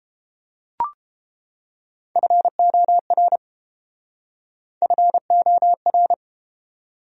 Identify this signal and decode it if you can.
Morse code, heard as a keyed tone, about 30 wpm, 700 Hz. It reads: FOR FOR